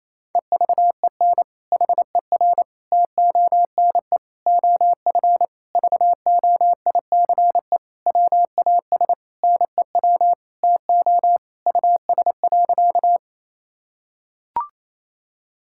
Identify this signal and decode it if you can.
Morse code, heard as a keyed tone, about 28 wpm, 700 Hz. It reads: EVEN HER TONE OF VOICE WAS NEW TO US.
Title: Sentences from Top 1600 Words 28wpm